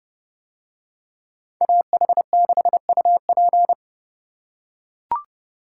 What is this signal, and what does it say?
Morse code, heard as a keyed tone, about 28 wpm, 700 Hz. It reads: AH6UP